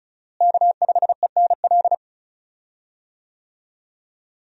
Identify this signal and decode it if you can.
Morse code, heard as a keyed tone, about 36 wpm, 700 Hz. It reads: K5ENL
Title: USA Call Signs - 5 Characters - Repeated 3x - 35wpm